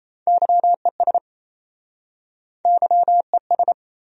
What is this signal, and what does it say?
Morse code, heard as a keyed tone, about 39 wpm, 700 Hz. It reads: YES YES